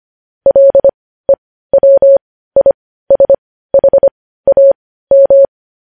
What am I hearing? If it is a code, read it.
Morse code, heard as a keyed tone, about 25 wpm, spaced as for 15 wpm, 550 Hz. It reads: LEWISHAM
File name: podcast / News headlines-1x-25x15